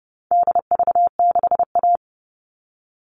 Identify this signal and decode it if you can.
Morse code, heard as a keyed tone, about 30 wpm, 700 Hz. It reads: DV6A
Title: All Cut Numbers - 4-Digit 30wpm